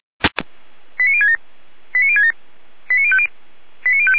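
Five-tone selective call (ZVEI), heard as quick runs of five tones, tones of about 70 ms. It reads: 89067 89067 89050 89050
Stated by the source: ZVEI five-tone call received by SDR.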